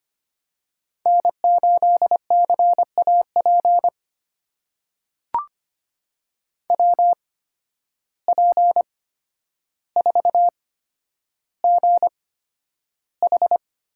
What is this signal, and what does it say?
Morse code, heard as a keyed tone, about 25 wpm, spaced as for 11 wpm, 700 Hz. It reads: N8CAP WP4GH
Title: USA Call Signs - 5 Characters - 8x Character Spacing - 25wpm